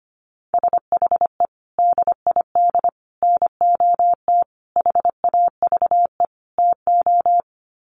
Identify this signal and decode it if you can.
Morse code, heard as a keyed tone, about 25 wpm, 700 Hz. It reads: SHE DID NOT HAVE TO
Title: Sets of 5 Words Encore2 25wpm